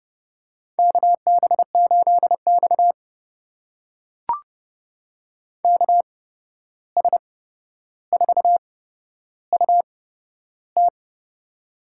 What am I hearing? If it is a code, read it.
Morse code, heard as a keyed tone, about 32 wpm, spaced as for 12 wpm, 700 Hz. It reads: KB8X KS4UT
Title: USA Call Signs - 8x Character Spacing - 30wpm